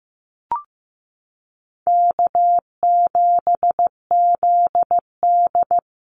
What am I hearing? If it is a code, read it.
Morse code, heard as a keyed tone, about 15 wpm, 700 Hz. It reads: K7ZD
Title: Easy Call Signs Repeated 3x - 15wpm